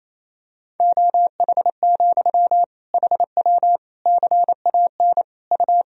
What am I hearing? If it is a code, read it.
Morse code, heard as a keyed tone, about 28 wpm, 700 Hz. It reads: OH, HW CAN U
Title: Sentences from Top 500 Words and Abbreviations 28wpm